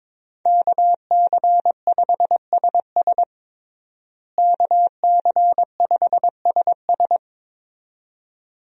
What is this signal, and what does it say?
Morse code, heard as a keyed tone, about 22 wpm, 700 Hz. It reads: KC5SS KC5SS
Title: USA Call Signs - 5 Characters - Repeated 2x - 22wpm